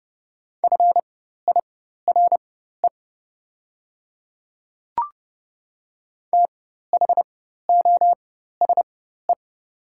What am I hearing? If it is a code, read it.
Morse code, heard as a keyed tone, about 30 wpm, spaced as for 14 wpm, 700 Hz. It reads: FIRE THOSE
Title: 4x Character Spacing - Top 500 Words - 30wpm